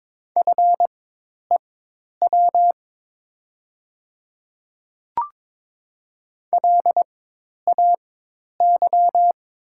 Morse code, heard as a keyed tone, about 22 wpm, spaced as for 10 wpm, 700 Hz. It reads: FEW LAY
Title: 4x Character Spacing - Top 500 Words - 22wpm